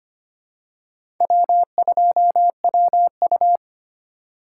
Morse code, heard as a keyed tone, about 25 wpm, 700 Hz. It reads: W2WU
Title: USA Call Signs - 4x Character Spacing - 25wpm